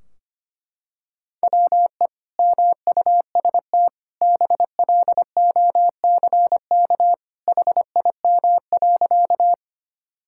Morse code, heard as a keyed tone, about 25 wpm, 700 Hz. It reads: WE MUST BLOCK HIM.
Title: Sentences from Top 1600 Words 25wpm